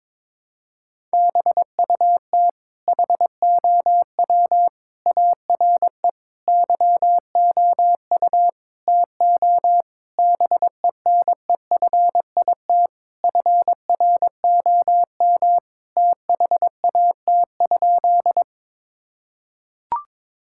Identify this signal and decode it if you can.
Morse code, heard as a keyed tone, about 22 wpm, 700 Hz. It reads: BUT HOW ARE YOU TO BENEFIT FROM THAT?
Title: Sentences from Top 1000 Words - Review Part 6 - 22wpm